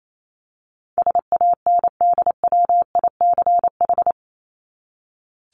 Morse code, heard as a keyed tone, about 28 wpm, 700 Hz. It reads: SANDWICH
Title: Top 1751-2000 Words 28wpm